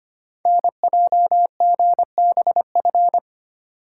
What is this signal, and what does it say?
Morse code, heard as a keyed tone, about 25 wpm, 700 Hz. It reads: NJGBF
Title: Code Groups - 5 Letters - 25wpm